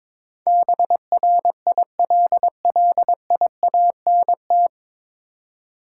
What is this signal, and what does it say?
Morse code, heard as a keyed tone, about 22 wpm, 700 Hz. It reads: BRILLIANT
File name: 9 Letter Words - First 3 Letters Given - 22wpm